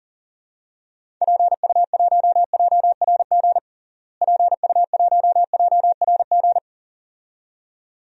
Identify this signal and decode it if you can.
Morse code, heard as a keyed tone, about 40 wpm, 700 Hz. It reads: PU1JRG PU1JRG
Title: DXCC-100 Call Signs - 6 Characters - Repeated 2x - 40wpm